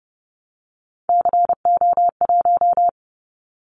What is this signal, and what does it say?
Morse code, heard as a keyed tone, about 30 wpm, 700 Hz. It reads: CO1
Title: Callsign Prefixes for Top 100 DXCC - 30wpm